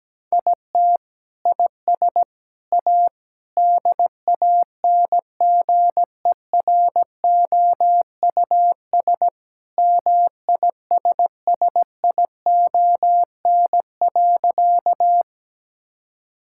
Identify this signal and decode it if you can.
Morse code, heard as a keyed tone, about 17 wpm, 700 Hz. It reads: IT IS A DANGEROUS MISSION.
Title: Sentences from Top 1600 Words 17wpm